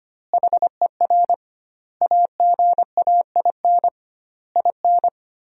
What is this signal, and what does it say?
Morse code, heard as a keyed tone, about 25 wpm, 700 Hz. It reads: HER AGAIN IN
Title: 2x Word-Spacing - Sentences from Top 200 Words - 25wpm